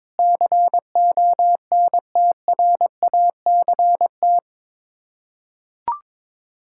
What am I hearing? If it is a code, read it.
Morse code, heard as a keyed tone, about 21 wpm, 700 Hz. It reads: CONTRACT
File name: practice / 22wpm / Sets of 8 Words - 22wpm